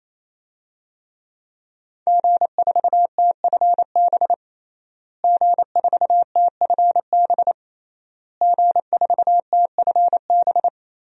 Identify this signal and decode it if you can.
Morse code, heard as a keyed tone, about 28 wpm, 700 Hz. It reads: G4TFB G4TFB G4TFB
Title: DXCC-100 Call Signs - 5 Characters - Repeated 3x - 28wpm